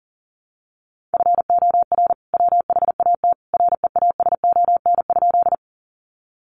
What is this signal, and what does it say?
Morse code, heard as a keyed tone, about 37 wpm, 700 Hz. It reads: FOR WHAT REASON?
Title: Sentences from Top 700 Words 40wpm